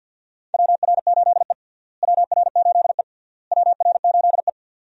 Morse code, heard as a keyed tone, about 48 wpm, 700 Hz. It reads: WR8E WR8E WR8E
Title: USA Call Signs - 4 Characters - Repeated 3x - 50wpm